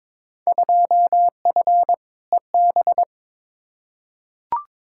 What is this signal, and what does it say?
Morse code, heard as a keyed tone, about 22 wpm, 700 Hz. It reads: 2F EB
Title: Field Day Exchanges Encore - 22wpm